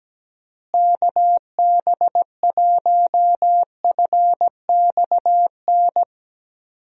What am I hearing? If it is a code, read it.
Morse code, heard as a keyed tone, about 17 wpm, 700 Hz. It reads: KB1FXN
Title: USA Call Signs - 6 Characters - Repeated 3x - 17wpm